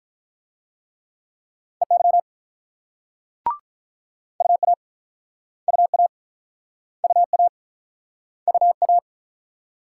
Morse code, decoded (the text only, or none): EY UA UA UA UA